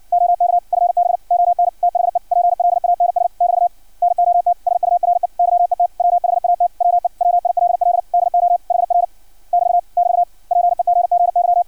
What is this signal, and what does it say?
Morse code, heard as a keyed tone, about 56 wpm, 700 Hz. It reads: OK FB ON THE PLANS <BT> NOT SURE YET WHAT WE WILL DO HR <BT> <BT> PEGGY
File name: archived / CRSnet